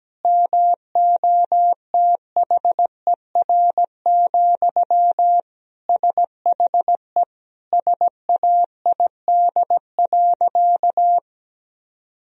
Morse code, heard as a keyed tone, about 17 wpm, 700 Hz. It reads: MOTHER, SHE SAID.
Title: Sentences from Top 1000 Words - Review Part 1 - 17wpm